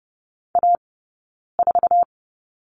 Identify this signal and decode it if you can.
Morse code, heard as a keyed tone, about 30 wpm, 700 Hz. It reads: A 4
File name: Lesson 20d - Number 3 - Two Characters 30wpm